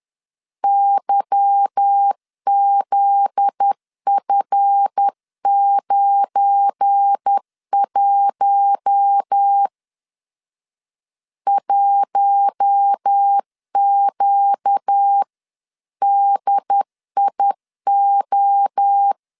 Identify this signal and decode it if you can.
Morse code, heard as a keyed tone, about 10 wpm, 800 Hz. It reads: YZF91 1QDIO
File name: CW / DET / 1 9 1500C S A 10wpm 4511